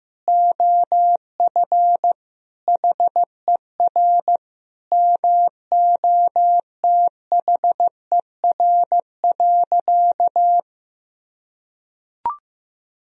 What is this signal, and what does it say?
Morse code, heard as a keyed tone, about 15 wpm, 700 Hz. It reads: OF HER MOTHER.
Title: Sentences from Top 1000 Words - Review Part 4 - 15wpm